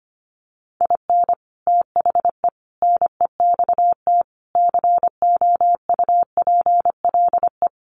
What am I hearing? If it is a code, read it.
Morse code, heard as a keyed tone, about 25 wpm, 700 Hz. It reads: IN THE NEXT COUPLE
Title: Sets of 5 Words Encore2 25wpm